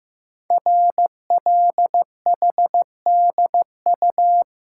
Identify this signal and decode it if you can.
Morse code, heard as a keyed tone, about 15 wpm, 700 Hz. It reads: RLHDU
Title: Code Groups - 5 Letters - 15wpm